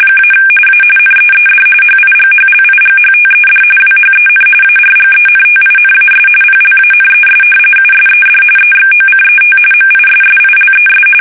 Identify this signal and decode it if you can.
RTTY, about 45.45 baud. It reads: - A NEED HIGHLIGHTED WHEN ARGENTINAS PLANES USED EXOCET MISSILES
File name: fsk850 / battleshipnj